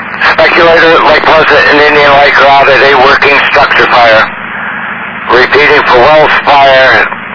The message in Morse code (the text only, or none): none